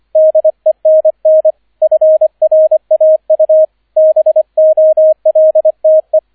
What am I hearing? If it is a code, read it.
Morse code, heard as a keyed tone, about 25 wpm, 630 Hz, with radio noise. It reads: DENN FRAU BOLTE